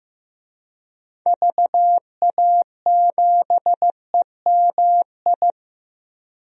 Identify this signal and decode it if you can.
Morse code, heard as a keyed tone, about 15 wpm, 700 Hz. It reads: VA7EMI
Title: DXCC-100 Call Signs - 6 Characters - Repeated 2x - 15wpm